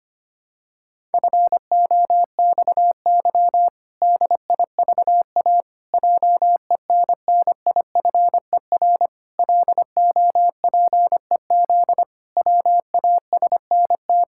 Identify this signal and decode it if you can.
Morse code, heard as a keyed tone, about 25 wpm, 700 Hz. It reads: FOXY DIVA JENNIFER LOPEZ WASNT
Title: Pangram Challenge v2 - 25wpm